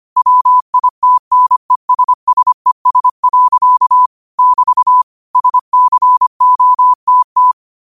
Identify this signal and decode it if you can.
Morse code, heard as a keyed tone, about 25 wpm, 1000 Hz. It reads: WITNESSES. <BT> SCOTT
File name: qotdfiles / 25wpm